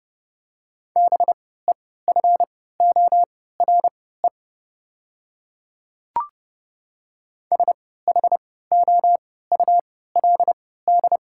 Morse code, heard as a keyed tone, about 30 wpm, spaced as for 17 wpm, 700 Hz. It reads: BEFORE SHOULD
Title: 3x Character Spacing - Top 300 Words - 30wpm